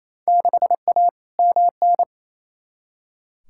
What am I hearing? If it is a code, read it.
Morse code, heard as a keyed tone, about 28 wpm, 700 Hz. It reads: 6A MN
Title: Field Day Exchanges Encore - 28wpm